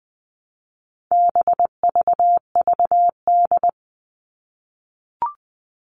Morse code, heard as a keyed tone, about 20 wpm, 700 Hz. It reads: BVVD
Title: All Cut Numbers - 4-Digit 20wpm